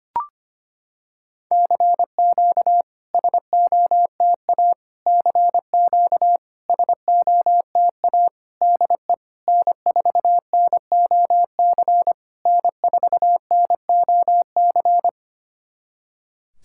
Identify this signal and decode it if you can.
Morse code, heard as a keyed tone, about 25 wpm, 690 Hz. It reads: CQ SOTA CQ SOTA DE N4NOC N4NOC
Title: SOTA QSO Practice 25wpm